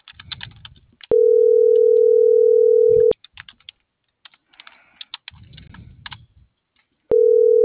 Ringback tone (North America), 440 + 480 Hz.